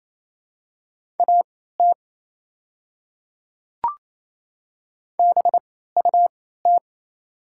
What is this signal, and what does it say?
Morse code, heard as a keyed tone, about 28 wpm, spaced as for 16 wpm, 690 Hz. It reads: AT BUT
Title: 3x Character Spacing - Top 100 Words - 28wpm